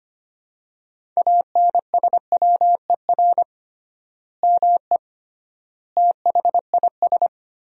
Morse code, heard as a keyed tone, about 25 wpm, 700 Hz. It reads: ANSWER ME THIS
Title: 3x Word-Spacing - Sentences from Top 400 Words - 25wpm